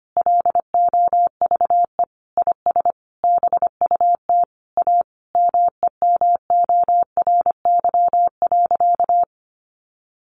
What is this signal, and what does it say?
Morse code, heard as a keyed tone, about 25 wpm, 700 Hz. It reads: LOVE IS BUT A MEMORY.
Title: Sentences from Top 1300 Words 25wpm